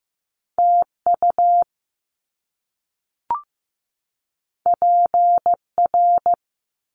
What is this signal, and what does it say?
Morse code, heard as a keyed tone, about 15 wpm, 700 Hz. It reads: TU PR
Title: Top Two Letter Combinations in Words 15wpm